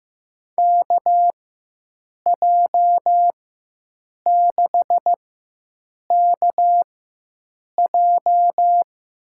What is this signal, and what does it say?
Morse code, heard as a keyed tone, about 15 wpm, spaced as for 7 wpm, 700 Hz. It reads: KJ6KJ
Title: USA Call Signs - 6 Characters - 4x Character Spacing - 15wpm